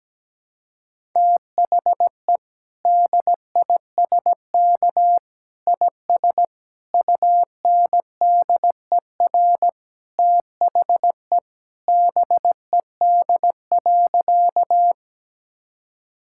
Morse code, heard as a keyed tone, about 17 wpm, 700 Hz. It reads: THE DISK IS UNDER THE BED.
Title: Sentences from Top 1000 Words - Review Part 6 - 17wpm